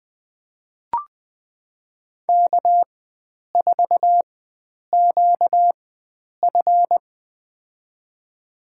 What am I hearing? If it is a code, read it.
Morse code, heard as a keyed tone, about 20 wpm, spaced as for 9 wpm, 700 Hz. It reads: K4QF
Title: USA Call Signs - 4 Characters - 4x Character Spacing - 20wpm